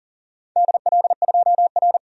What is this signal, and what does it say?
Morse code, heard as a keyed tone, about 40 wpm, 700 Hz. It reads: DL2R
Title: DXCC-100 Call Signs - 4 Characters - 40wpm